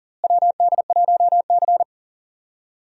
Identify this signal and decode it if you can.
Morse code, heard as a keyed tone, about 40 wpm, 700 Hz. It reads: WD1C
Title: USA Call Signs - 4 Characters - Repeated 2x - 40wpm